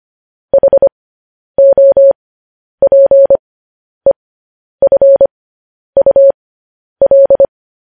Morse code, heard as a keyed tone, about 25 wpm, spaced as for 10 wpm, 550 Hz. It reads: HOPEFUL
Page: news-headlines-in-morse-code-at-10-wpm